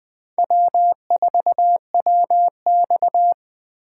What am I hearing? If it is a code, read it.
Morse code, heard as a keyed tone, about 20 wpm, 700 Hz. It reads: W4WX
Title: Speed Racing - USA 1x2 Call Signs 17wpm